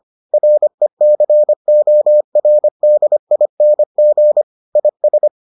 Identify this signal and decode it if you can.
Morse code, heard as a keyed tone, about 25 wpm, 600 Hz. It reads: RECORDING IS